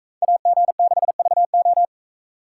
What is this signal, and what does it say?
Morse code, heard as a keyed tone, about 44 wpm, 700 Hz. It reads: AG6VO